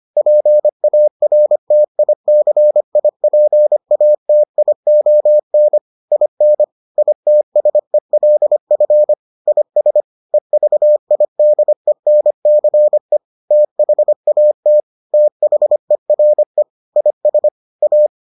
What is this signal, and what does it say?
Morse code, heard as a keyed tone, about 25 wpm, 600 Hz. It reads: PARTICIPATION IN ITSELF IS EVIDENCE THAT THERE IS A